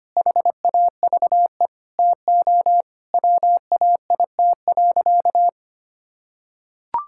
Morse code, heard as a keyed tone, about 25 wpm, 700 Hz. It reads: HAVE TO WAIT.
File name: practice / 25wpm / Sentences with 3-Word Phrases - 25wpm